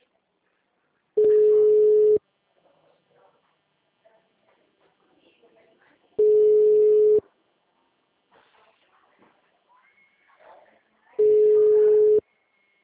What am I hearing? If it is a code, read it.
Ringback tone (Europe, 425 Hz), with speech.